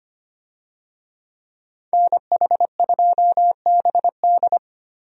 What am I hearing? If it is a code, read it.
Morse code, heard as a keyed tone, about 25 wpm, 700 Hz. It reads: NH2BD
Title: USA Call Signs - 5 Characters - 25wpm